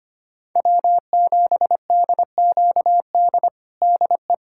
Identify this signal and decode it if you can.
Morse code, heard as a keyed tone, about 25 wpm, 700 Hz. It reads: W7DQD DE
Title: USA Callsign DE Callsign - 5 Characters - 25wpm